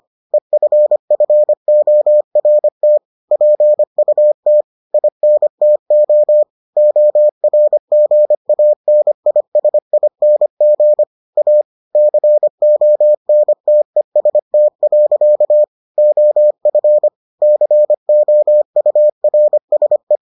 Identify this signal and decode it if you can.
Morse code, heard as a keyed tone, about 25 wpm, 600 Hz. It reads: EFFORT PUT INTO ORGANISING A CONTEST. OF COURSE